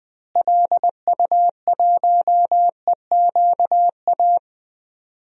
Morse code, heard as a keyed tone, about 20 wpm, 700 Hz. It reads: LU1EQA
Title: DXCC-100 Call Signs - 6 Characters - Repeated 3x - 20wpm